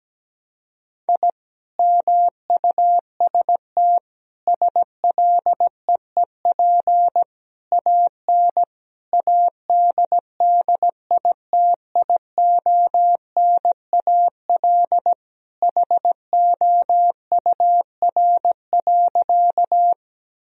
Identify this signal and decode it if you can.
Morse code, heard as a keyed tone, about 17 wpm, 700 Hz. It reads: I MUST SLEEP AN ADDITIONAL HOUR.
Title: Sentences from Top 1000 Words - Review Part 4 - 17wpm